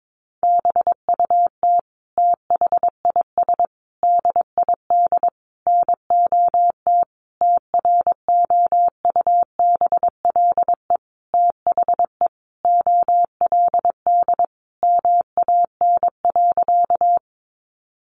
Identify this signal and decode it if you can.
Morse code, heard as a keyed tone, about 22 wpm, 700 Hz. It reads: BUT THIS DID NOT TROUBLE THE OLD MAN.
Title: Sentences from Top 1200 Words 22wpm